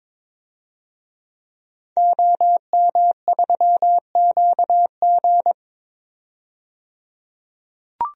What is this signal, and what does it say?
Morse code, heard as a keyed tone, about 22 wpm, 700 Hz. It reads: OM3QG
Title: DXCC-100 Call Signs - 5 Characters - 22wpm